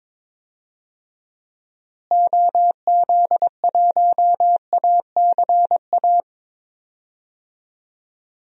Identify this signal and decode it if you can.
Morse code, heard as a keyed tone, about 21 wpm, 700 Hz. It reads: OZ1ACA
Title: DXCC-100 Call Signs - 6 Characters - Repeated 3x - 22wpm